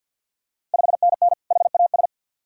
Morse code, heard as a keyed tone, about 49 wpm, 690 Hz. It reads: 5NN HAS
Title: Ontario QSO Party Exchange 50wpm